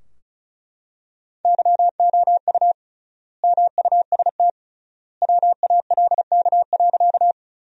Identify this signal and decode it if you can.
Morse code, heard as a keyed tone, about 36 wpm, 700 Hz. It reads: YOU MUST WALK.
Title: 3x Word-Spacing - Sentences from Top 300 Words - 35wpm